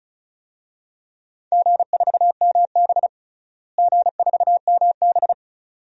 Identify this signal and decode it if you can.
Morse code, heard as a keyed tone, about 34 wpm, 700 Hz. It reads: G4MB G4MB
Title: DXCC-100 Call Signs - 4 Characters - Repeated 2x - 35wpm